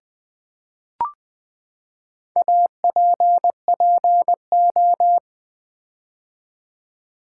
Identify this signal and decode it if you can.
Morse code, heard as a keyed tone, about 20 wpm, 700 Hz. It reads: APPO
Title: Build A Word - 4 Letters At A Time - 11-Letter Words - 20wpm